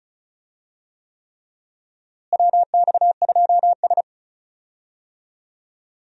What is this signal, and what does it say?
Morse code, heard as a keyed tone, about 36 wpm, 700 Hz. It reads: WX2S
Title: USA Call Signs - 4 Characters - Repeated 3x - 35wpm